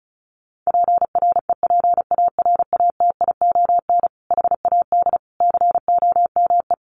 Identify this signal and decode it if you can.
Morse code, heard as a keyed tone, about 34 wpm, 700 Hz. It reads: PREPARATION HAD COME
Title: Sentences from Top 1400 Words 35wpm